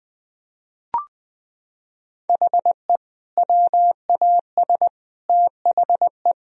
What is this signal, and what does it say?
Morse code, heard as a keyed tone, about 20 wpm, 700 Hz. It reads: HE WAS THE
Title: Sentences from Top 1000 Words - Review Part 1 - 20wpm